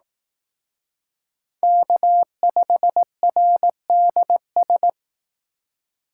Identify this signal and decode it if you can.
Morse code, heard as a keyed tone, about 18 wpm, 700 Hz. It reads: K5RDS
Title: Speed Racing - USA 1x3 Call Signs 15wpm